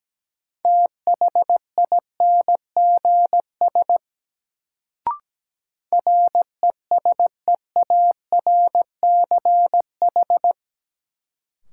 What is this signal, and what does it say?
Morse code, heard as a keyed tone, about 17 wpm, 700 Hz. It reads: THINGS RESEARCH
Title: Sentences from Top 1000 Words - Review Part 4 - New Word Drill - 17wpm